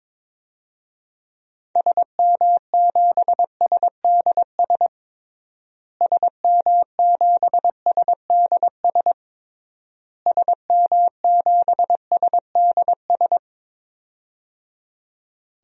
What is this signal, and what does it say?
Morse code, heard as a keyed tone, about 22 wpm, 700 Hz. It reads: SM7SDS SM7SDS SM7SDS
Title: DXCC-100 Call Signs - 6 Characters - Repeated 3x - 22wpm